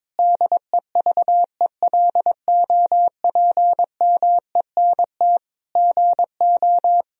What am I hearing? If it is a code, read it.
Morse code, heard as a keyed tone, about 22 wpm, 700 Hz. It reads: DEVELOPMENT GO
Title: Sentences from Top 1000 Words - Review Part 4 - 22wpm